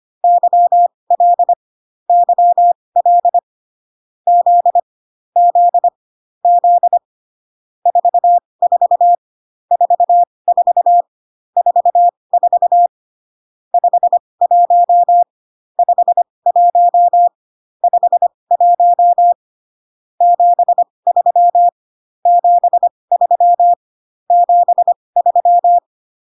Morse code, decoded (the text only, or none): YL YL Z Z Z 44 44 44 51 51 51 73 73 73